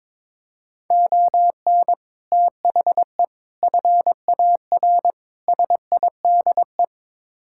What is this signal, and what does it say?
Morse code, heard as a keyed tone, about 22 wpm, 700 Hz. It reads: ON THE FAR SIDE
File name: practice / 22wpm / Build A Phrase - 5-Word Phrases - 22wpm